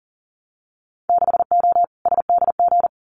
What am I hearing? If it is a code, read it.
Morse code, heard as a keyed tone, about 38 wpm, 700 Hz. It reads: 6O SDG
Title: Winter Field Day Exchanges Encore 40wpm